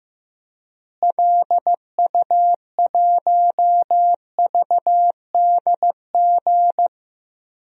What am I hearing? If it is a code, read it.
Morse code, heard as a keyed tone, about 15 wpm, 700 Hz. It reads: LU1VDG
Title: DXCC-100 Call Signs - 6 Characters - Repeated 2x - 15wpm